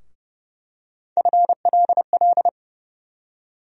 Morse code, heard as a keyed tone, about 30 wpm, 700 Hz. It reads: FLL
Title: USA Call Sign Postfixes - 3 Letters 30wpm